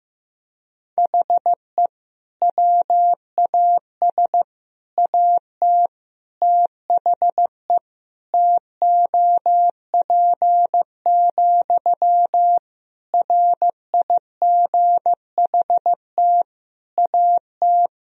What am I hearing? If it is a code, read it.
Morse code, heard as a keyed tone, about 16 wpm, 700 Hz. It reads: HE WAS AT THE TOP, RIGHT AT
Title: Sentences from Top 1000 Words - Review Part 2 - 15wpm